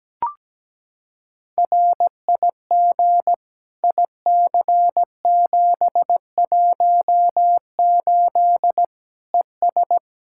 Morse code, heard as a keyed tone, about 17 wpm, 700 Hz. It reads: RIG IC718 ES
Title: Practice QSO - Exchange 2 - Info - 17wpm